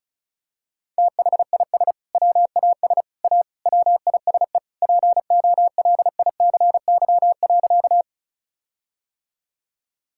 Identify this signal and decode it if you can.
Morse code, heard as a keyed tone, about 34 wpm, 700 Hz. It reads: THIS WAS A WISE POLICY.
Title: Sentences from Top 1600 Words 35wpm